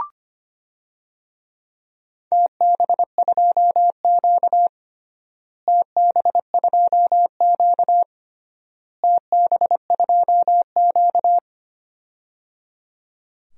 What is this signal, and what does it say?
Morse code, heard as a keyed tone, about 25 wpm, 700 Hz. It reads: TB2Q TB2Q TB2Q
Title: DXCC-100 Call Signs - 4 Characters - Repeated 3x - 25wpm